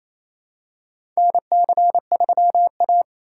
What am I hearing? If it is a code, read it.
Morse code, heard as a keyed tone, about 28 wpm, 700 Hz. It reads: NC3A
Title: USA Call Signs - 2x Character Spacing - 28wpm